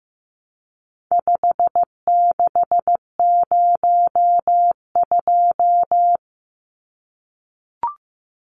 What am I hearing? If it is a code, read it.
Morse code, heard as a keyed tone, about 14 wpm, 700 Hz. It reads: 5602
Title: Numbers - 4-Digit 15wpm